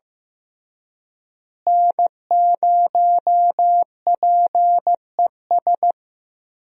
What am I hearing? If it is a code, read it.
Morse code, heard as a keyed tone, about 15 wpm, 700 Hz. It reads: N0PES
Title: USA Call Signs - 5 Characters - Repeated 3x - 15wpm